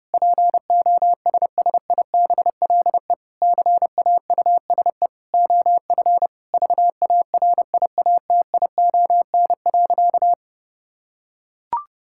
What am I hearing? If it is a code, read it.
Morse code, heard as a keyed tone, about 30 wpm, 700 Hz. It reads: POSSIBLE CAUSE OF VARIATION.
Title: Sentences from Top 1600 Words 30wpm